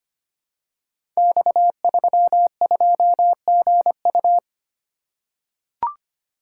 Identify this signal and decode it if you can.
Morse code, heard as a keyed tone, about 25 wpm, 700 Hz. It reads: X32GU
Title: Code Groups - 5 Letter:Numbers - 25wpm